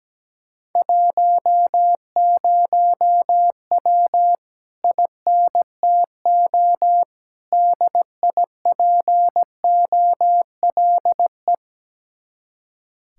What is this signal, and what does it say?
Morse code, heard as a keyed tone, about 17 wpm, 700 Hz. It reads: 10W INTO DIPOLE
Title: Speed-Racing Common QSO Phrases - 17wpm